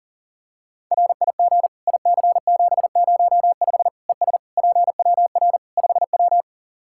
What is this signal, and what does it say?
Morse code, heard as a keyed tone, about 40 wpm, 700 Hz. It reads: RIG IC705 ES PWR 5W
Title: Practice QSO - Exchange 2 - Info - 40wpm